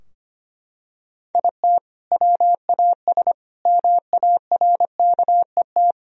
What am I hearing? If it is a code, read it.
Morse code, heard as a keyed tone, about 25 wpm, 700 Hz. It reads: IT WAS MARKET
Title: Sentences from Top 1000 Words - Review Part 3 - 25wpm